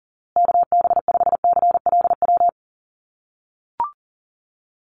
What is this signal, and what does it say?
Morse code, heard as a keyed tone, about 40 wpm, 700 Hz. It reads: KB5CLW
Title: USA Call Signs Repeated 2x 40wpm